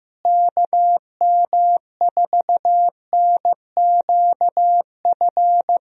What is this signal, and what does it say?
Morse code, heard as a keyed tone, about 15 wpm, 700 Hz. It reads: KM4NQF
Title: USA Call Signs - 6 Characters - Repeated 3x - 15wpm